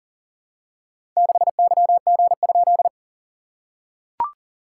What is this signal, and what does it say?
Morse code, heard as a keyed tone, about 37 wpm, 690 Hz. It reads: 6YG?
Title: SOTA QSO Practice 40wpm